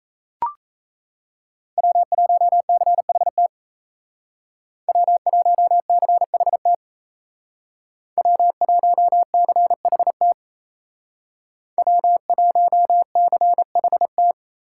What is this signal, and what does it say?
Morse code, heard as a keyed tone, about 36 wpm, 700 Hz. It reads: W1CHT W1CHT W1CHT W1CHT